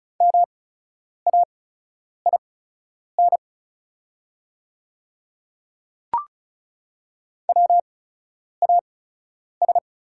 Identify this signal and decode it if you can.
Morse code, heard as a keyed tone, about 36 wpm, spaced as for 10 wpm, 700 Hz. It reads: MAIN WAS
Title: 8x Character Spacing - Top 300 Words - 35wpm